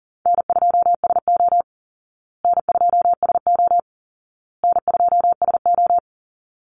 Morse code, heard as a keyed tone, about 40 wpm, 700 Hz. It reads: N2SO N2SO N2SO
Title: Easy Call Signs Repeated 3x - 40wpm